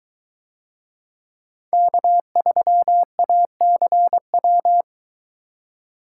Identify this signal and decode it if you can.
Morse code, heard as a keyed tone, about 23 wpm, 700 Hz. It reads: K3ACW